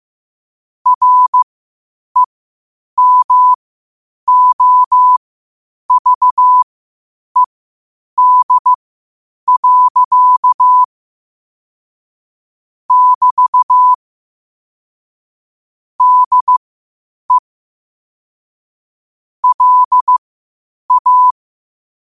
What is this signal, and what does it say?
Morse code, heard as a keyed tone, about 15 wpm, spaced as for 8 wpm, 1000 Hz. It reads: REMOVED. <BT> DE LA
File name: qotdfiles / 8wpm